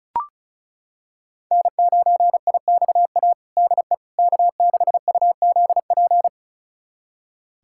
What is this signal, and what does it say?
Morse code, heard as a keyed tone, about 36 wpm, 700 Hz. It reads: N9IXA DE K6UZP
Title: USA Callsign DE Callsign - 5 Characters - 35wpm